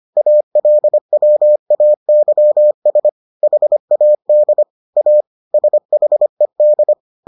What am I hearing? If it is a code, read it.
Morse code, heard as a keyed tone, about 25 wpm, 600 Hz. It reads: ALWAYS HAD A SHED